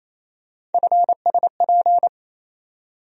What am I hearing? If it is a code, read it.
Morse code, heard as a keyed tone, about 28 wpm, 700 Hz. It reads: FSP
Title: USA Call Sign Postfixes - 3 Letters 28wpm